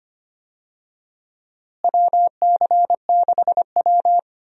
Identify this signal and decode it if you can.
Morse code, heard as a keyed tone, about 25 wpm, 700 Hz. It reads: WC6W
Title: USA Call Signs - 4 Characters - Repeated 2x - 25wpm